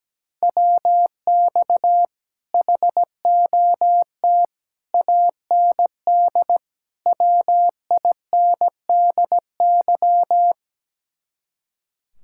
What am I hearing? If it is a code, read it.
Morse code, heard as a keyed tone, about 17 wpm, 700 Hz. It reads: WX HOT AND WINDY
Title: Speed-Racing Common QSO Phrases - 17wpm